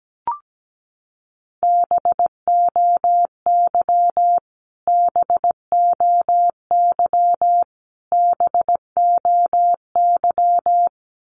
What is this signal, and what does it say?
Morse code, heard as a keyed tone, about 17 wpm, 700 Hz. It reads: BOY BOY BOY
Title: Top 500 Words Repeated 17wpm